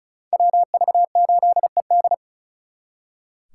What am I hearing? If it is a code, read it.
Morse code, heard as a keyed tone, about 43 wpm, 700 Hz. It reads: WV8ED
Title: Speed Racing - USA 2x2 Call Signs 35wpm